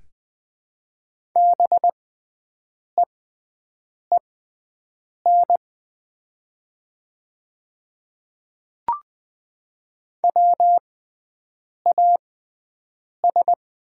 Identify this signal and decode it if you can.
Morse code, heard as a keyed tone, about 20 wpm, spaced as for 7 wpm, 690 Hz. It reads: BEEN WAS
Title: 6x Character Spacing - Top 200 Words - 20wpm